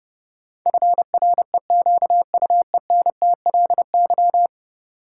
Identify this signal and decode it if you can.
Morse code, heard as a keyed tone, about 30 wpm, 700 Hz. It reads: FREQUENTLY
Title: Sentences from Top 1300 Words - New Word Drill - 30wpm